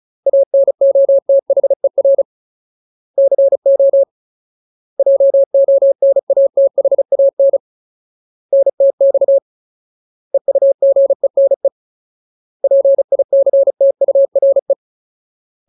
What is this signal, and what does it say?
Morse code, heard as a keyed tone, about 34 wpm, 550 Hz. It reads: ANOTHER CO JONATHAN NTX EUGENE PICTURE
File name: Ultimate Word List / 35WPM Ultimate